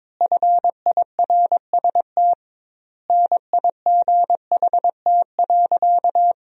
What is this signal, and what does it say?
Morse code, heard as a keyed tone, about 22 wpm, 700 Hz. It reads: FIRST NIGHT.
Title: 2x Word-Spacing - Sentences from Top 300 Words - 22wpm